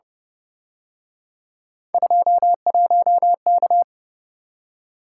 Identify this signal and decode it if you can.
Morse code, heard as a keyed tone, about 31 wpm, 700 Hz. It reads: 21K
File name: Speed-Racing - Three Letter-Number Combinations - 25wpm